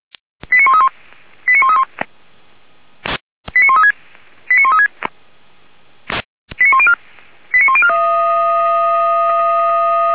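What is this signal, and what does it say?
Five-tone selective call (ZVEI), heard as quick runs of five tones, tones of about 80 ms. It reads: 89131 89131 89136 89136 89104 89104
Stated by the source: ZVEI five-tone selective call 89227 received on SDR, on the Feuerwehr Orlamünde (f85360) audio server.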